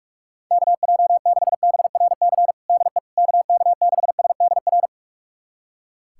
Morse code, heard as a keyed tone, about 45 wpm, 700 Hz. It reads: KJ6BRC DE KK6SDR
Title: USA Callsign DE Callsign - 6 Characters - 45wpm